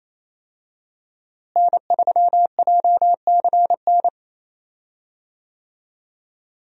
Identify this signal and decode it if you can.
Morse code, heard as a keyed tone, about 28 wpm, 700 Hz. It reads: N3JCN
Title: USA Call Signs - 5 Characters - Repeated 2x - 28wpm